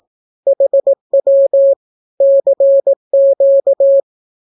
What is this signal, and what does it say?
Morse code, heard as a keyed tone, about 18 wpm, 550 Hz. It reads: HW CQ
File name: naukapodstaw / skroty / skroty